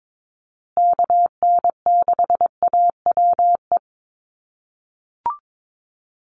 Morse code, heard as a keyed tone, about 22 wpm, 700 Hz. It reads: KN6AWE
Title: USA Call Signs Repeated 3x 22wpm